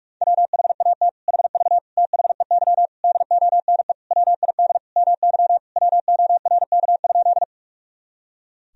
Morse code, heard as a keyed tone, about 46 wpm, 700 Hz. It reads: WHAT HV THEY DONE WID MY WORK?
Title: Sentences from Top 500 Words and Abbreviations 45wpm